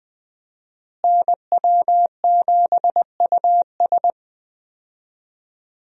Morse code, heard as a keyed tone, about 20 wpm, 700 Hz. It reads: NW7US
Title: USA Call Signs - 5 Characters - Repeated 3x - 20wpm